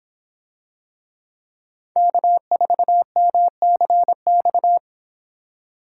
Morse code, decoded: K4MCX

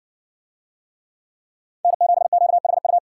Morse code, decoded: N7QHU

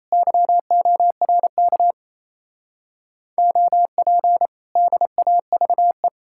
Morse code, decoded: YORK OP DAVE

